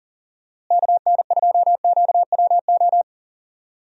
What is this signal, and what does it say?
Morse code, heard as a keyed tone, about 38 wpm, 700 Hz. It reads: KN2QWO
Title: USA Call Signs - 6 Characters - Repeated 2x - 40wpm